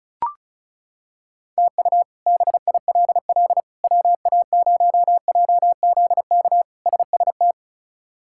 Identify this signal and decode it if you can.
Morse code, heard as a keyed tone, about 36 wpm, 700 Hz. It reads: TU BILL WA0JZK SST
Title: K1USN SST QSO Practice - 35wpm